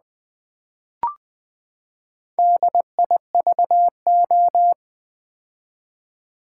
Morse code, heard as a keyed tone, about 20 wpm, 700 Hz. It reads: DIVO
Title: Build A Word - 4 Letters At A Time - 8-Letter Words - 20wpm